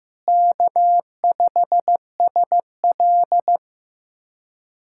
Morse code, heard as a keyed tone, about 15 wpm, 700 Hz. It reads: K5SL
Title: USA Callsign DE Callsign - 4 Characters - 15wpm